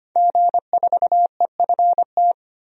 Morse code, heard as a keyed tone, about 25 wpm, 700 Hz. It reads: G4EFT
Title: DXCC-100 Call Signs - 5 Characters - 25wpm